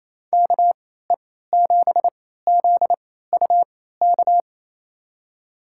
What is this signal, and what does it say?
Morse code, heard as a keyed tone, about 27 wpm, spaced as for 16 wpm, 700 Hz. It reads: KE7ZUK
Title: USA Call Signs - 6 Characters - 3x Character Spacing - 28wpm